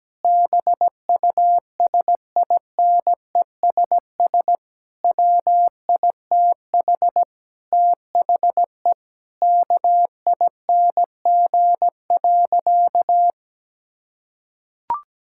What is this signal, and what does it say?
Morse code, heard as a keyed tone, about 17 wpm, 700 Hz. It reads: BUSINESS WITH THE KING.
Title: Sentences from Top 1000 Words - Review Part 6 - 17wpm